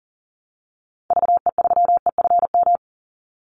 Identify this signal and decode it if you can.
Morse code, heard as a keyed tone, about 40 wpm, 700 Hz. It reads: VE3EFM
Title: Call Signs - 40wpm